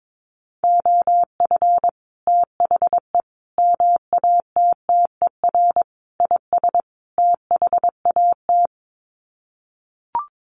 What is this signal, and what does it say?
Morse code, heard as a keyed tone, about 22 wpm, 700 Hz. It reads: OF THE MATTER IS THAT
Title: Sets of 5 Words Encore 22wpm